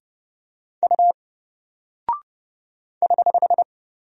Morse code, heard as a keyed tone, about 30 wpm, 700 Hz. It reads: U <HH>
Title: Single Character (Letter:Number:Punctuation:Prosign) Alt 30wpm